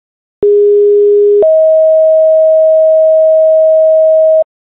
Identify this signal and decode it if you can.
Two-tone page, heard as a one-second tone, then a three-second tone, about 410 then 640 Hz, with noise. It is heard from 0.4 to 4.4 s.